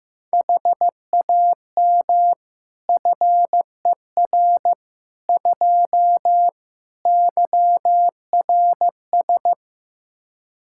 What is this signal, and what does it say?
Morse code, heard as a keyed tone, about 15 wpm, 700 Hz. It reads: HAM FER 2 YRS
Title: Practice QSO - Exchange 3 - More Info - 15wpm